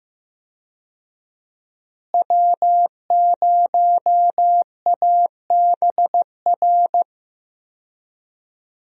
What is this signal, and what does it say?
Morse code, heard as a keyed tone, about 15 wpm, 700 Hz. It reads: W0ABR
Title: USA Call Signs - 5 Characters - Repeated 3x - 15wpm